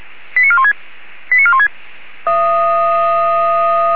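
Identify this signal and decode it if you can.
Five-tone selective call (ZVEI), heard as quick runs of five tones, tones of about 70 ms. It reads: 89416 89416
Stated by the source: ZVEI five-tone selective call received by SDR.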